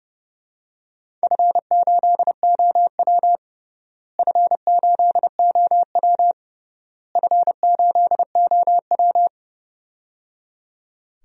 Morse code, heard as a keyed tone, about 30 wpm, 700 Hz. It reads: F8OW F8OW F8OW
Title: DXCC-100 Call Signs - 4 Characters - Repeated 3x - 30wpm